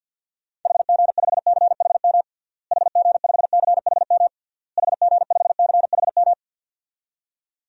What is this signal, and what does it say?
Morse code, heard as a keyed tone, about 52 wpm, 700 Hz. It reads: HG5CHM HG5CHM HG5CHM
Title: DXCC-100 Call Signs - 6 Characters - Repeated 3x - 50wpm